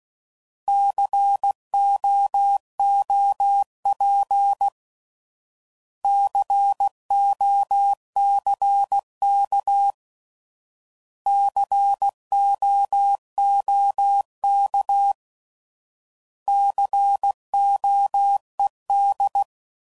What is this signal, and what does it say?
Morse code, heard as a keyed tone, about 16 wpm, 800 Hz. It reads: COOP COCK COOK COED
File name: Prefixes / By First Letter / Cs